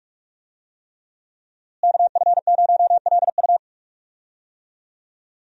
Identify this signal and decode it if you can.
Morse code, heard as a keyed tone, about 46 wpm, 700 Hz. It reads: KF0LU
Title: USA Call Signs - 5 Characters - Repeated 3x - 45wpm